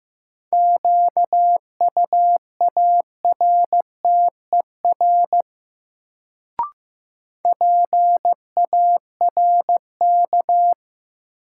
Morse code, heard as a keyed tone, about 15 wpm, 700 Hz. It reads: QUARTER PARK
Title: Sentences from Top 1300 Words - New Word Drill Alt - 15wpm